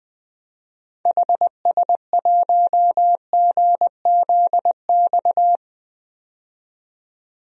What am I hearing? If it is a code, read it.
Morse code, heard as a keyed tone, about 20 wpm, 700 Hz. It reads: HS1GZX
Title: DXCC-100 Call Signs - 6 Characters - 20wpm